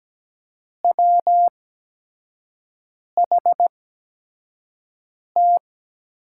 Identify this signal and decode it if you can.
Morse code, heard as a keyed tone, about 17 wpm, spaced as for 5 wpm, 700 Hz. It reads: WHT
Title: USA Call Signs - 8x Character Spacing - 17wpm